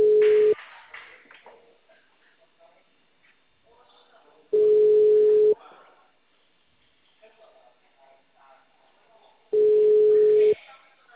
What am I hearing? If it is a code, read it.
Ringback tone (Europe, 425 Hz), with speech.